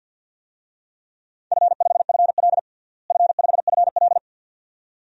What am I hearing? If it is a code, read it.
Morse code, heard as a keyed tone, about 47 wpm, 700 Hz. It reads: F5FL F5FL
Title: DXCC-100 Call Signs - 4 Characters - Repeated 2x - 50wpm